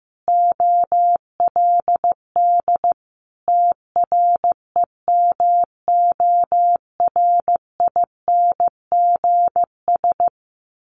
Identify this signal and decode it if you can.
Morse code, heard as a keyed tone, about 15 wpm, 700 Hz. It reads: OLD TREMORINGS
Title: A Sound Of Thunder 15wpm